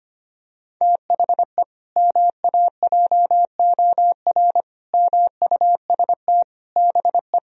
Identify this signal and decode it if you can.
Morse code, heard as a keyed tone, about 25 wpm, 700 Hz. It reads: THE MAJOR MUST BE
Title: Sentences from Top 1000 Words - Review Part 4 - 25wpm